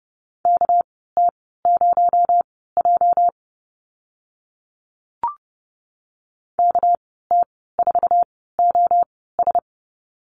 Morse code, heard as a keyed tone, about 30 wpm, spaced as for 17 wpm, 700 Hz. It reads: KT0J KT4OS
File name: Lesson 37g - Letter x - Call Signs 30wpm